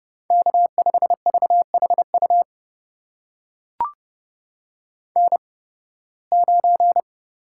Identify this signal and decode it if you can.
Morse code, heard as a keyed tone, about 31 wpm, spaced as for 20 wpm, 700 Hz. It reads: K5VHU N9
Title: USA Call Signs - 8x Character Spacing - 30wpm